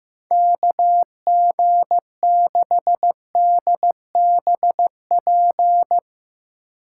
Morse code, heard as a keyed tone, about 15 wpm, 700 Hz. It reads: KG6DBP
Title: USA Call Signs - 6 Characters - 2x Character Spacing - 15wpm